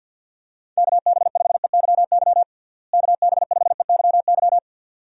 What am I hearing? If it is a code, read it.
Morse code, heard as a keyed tone, about 51 wpm, 700 Hz. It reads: KB5EYY KB5EYY
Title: USA Call Signs - 6 Characters - Repeated 2x - 50wpm